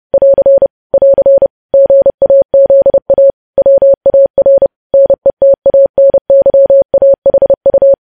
Morse code, heard as a keyed tone, about 30 wpm, 550 Hz. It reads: <AR> <AR> GAZA WAR NETANYAHU
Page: news headlines in morse code at 30 wpm